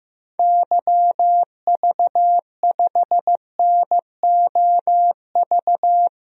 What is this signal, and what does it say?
Morse code, heard as a keyed tone, about 15 wpm, 700 Hz. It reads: YV5NOV
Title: DXCC-100 Call Signs - 6 Characters - 15wpm